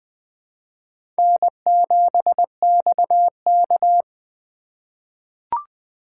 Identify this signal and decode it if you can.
Morse code, heard as a keyed tone, about 20 wpm, 700 Hz. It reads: N7XK
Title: USA Call Signs - 4 Characters - 8x Character Spacing - 20wpm